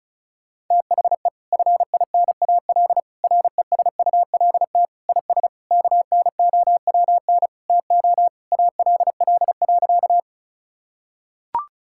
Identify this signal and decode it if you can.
Morse code, heard as a keyed tone, about 36 wpm, 700 Hz. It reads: THE FINAL RESULT IS KNOWN TO ALL.
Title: Sentences from Top 1000 Words - Review Part 5 - 35wpm